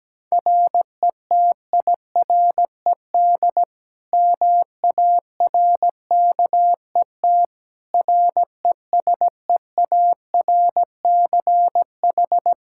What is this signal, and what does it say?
Morse code, heard as a keyed tone, about 17 wpm, 700 Hz. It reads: RETIRED MARKET RESEARCH
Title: Practice QSO - Exchange 3 - More Info - 17wpm